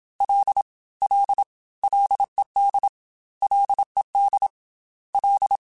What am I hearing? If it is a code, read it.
Morse code, heard as a keyed tone, about 27 wpm, 800 Hz. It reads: L L LED LED L
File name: F and L / F and L mp3 / F and L Prefixes 3x2 Twice @25